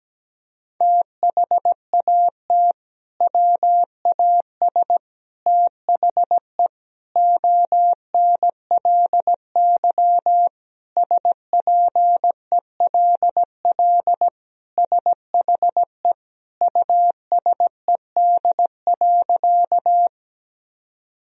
Morse code, heard as a keyed tone, about 17 wpm, 700 Hz. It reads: THAT WAS THE ONLY SPELL SHE USED.
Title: Sentences from Top 1000 Words - Review Part 1 - 17wpm